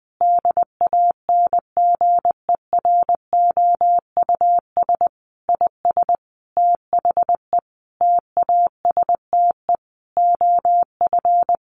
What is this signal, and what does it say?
Morse code, heard as a keyed tone, about 20 wpm, 700 Hz. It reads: DANGEROUS IS THE TASTE OF
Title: Sentences from Top 1200 Words 20wpm